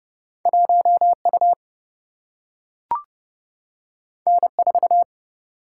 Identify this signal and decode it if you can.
Morse code, heard as a keyed tone, about 29 wpm, 700 Hz. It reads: 1U N4
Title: Single-Letter Single-Number 30wpm